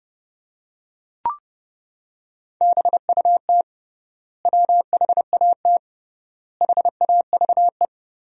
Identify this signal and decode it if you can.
Morse code, heard as a keyed tone, about 30 wpm, 700 Hz. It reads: BUT WHAT HAVE
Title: 3x Word-Spacing - Sentences from Top 100 Words - 30wpm